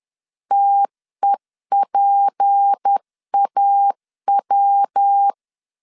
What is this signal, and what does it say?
Morse code, heard as a keyed tone, about 10 wpm, 800 Hz. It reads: TEPAW
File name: LEZIONI / 7 GF9Z / 7 G 10wpm 2501